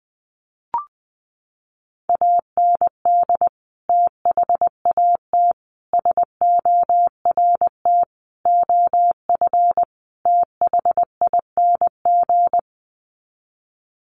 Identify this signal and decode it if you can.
Morse code, heard as a keyed tone, about 20 wpm, 700 Hz. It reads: AND THAT SORT OF THING
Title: Sets of 5 Words Encore2 20wpm